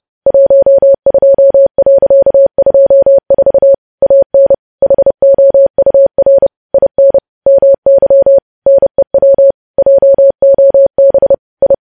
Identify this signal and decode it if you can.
Morse code, heard as a keyed tone, about 30 wpm, 550 Hz. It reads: 12.24 AN HOUR IN MY NEW JOB I
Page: news headlines in morse code at 30 wpm